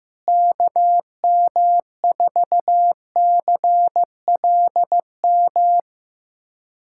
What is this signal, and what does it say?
Morse code, heard as a keyed tone, about 16 wpm, 700 Hz. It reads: KM4CLM
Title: USA Callsign DE Callsign - 6 Characters - 15wpm